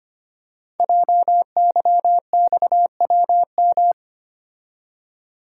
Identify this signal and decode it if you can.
Morse code, heard as a keyed tone, about 26 wpm, 700 Hz. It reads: JYXWM
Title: Code Groups - 5 Letters - 25wpm